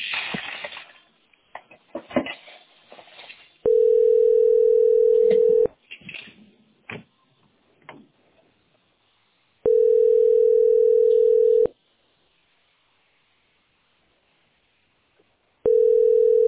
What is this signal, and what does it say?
Ringback tone (North America), 440 + 480 Hz.